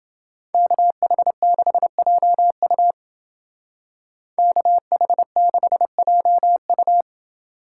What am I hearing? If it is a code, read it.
Morse code, heard as a keyed tone, about 28 wpm, 700 Hz. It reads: KH6JU KH6JU